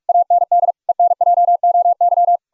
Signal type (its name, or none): Morse code